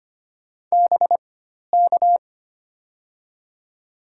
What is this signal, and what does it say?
Morse code, heard as a keyed tone, about 25 wpm, spaced as for 12 wpm, 700 Hz. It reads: BK
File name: Top 90 QSO Elements - 4x Character Spacing - 25wpm